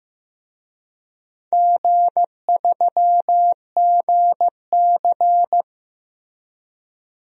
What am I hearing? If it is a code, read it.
Morse code, heard as a keyed tone, about 15 wpm, 700 Hz. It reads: G3GC
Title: DXCC-100 Call Signs - 4 Characters - Repeated 2x - 15wpm